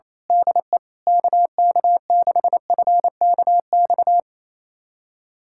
Morse code, heard as a keyed tone, about 28 wpm, 700 Hz. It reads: DE KK6FKX
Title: USA Callsign DE Callsign - 6 Characters - 28wpm